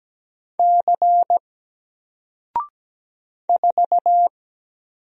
Morse code, heard as a keyed tone, about 17 wpm, 700 Hz. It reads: C 4
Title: Single Letter-Number Alt 17wpm